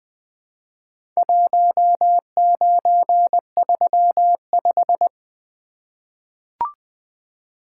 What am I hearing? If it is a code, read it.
Morse code, heard as a keyed tone, about 20 wpm, 700 Hz. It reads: 1935
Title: Four Letter-Number Combinations - 20wpm